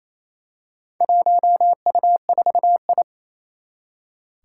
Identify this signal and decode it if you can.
Morse code, heard as a keyed tone, about 27 wpm, 700 Hz. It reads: 1U4I
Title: Four Challenging Letter-Number Combinations 28wpm